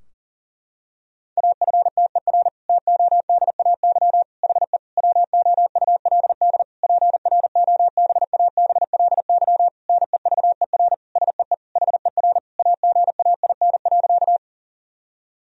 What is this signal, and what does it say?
Morse code, heard as a keyed tone, about 37 wpm, 700 Hz. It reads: AFTER TODAY HE WOULD PROBABLY NEVER SEE HER AGAIN.
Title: Sentences from Top 1000 Words - Review Part 4 - 40wpm